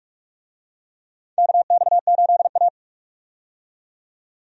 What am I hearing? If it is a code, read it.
Morse code, heard as a keyed tone, about 46 wpm, 700 Hz. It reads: KX8A
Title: USA Call Signs - 4 Characters - 45wpm